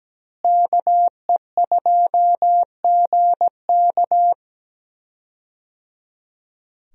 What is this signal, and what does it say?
Morse code, heard as a keyed tone, about 17 wpm, 700 Hz. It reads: KE2GK
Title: USA Call Signs - 5 Characters - Repeated 3x - 17wpm